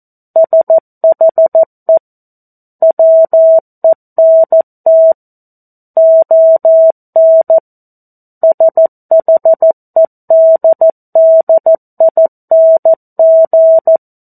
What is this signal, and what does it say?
Morse code, heard as a keyed tone, about 14 wpm, 650 Hz. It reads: SHE WENT ON SHEDDING
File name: 02-Chapter-02 14 WPM